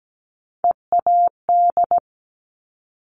Morse code, heard as a keyed tone, about 17 wpm, 700 Hz. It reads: EAD
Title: Top Three Letter Combinations in Words 17wpm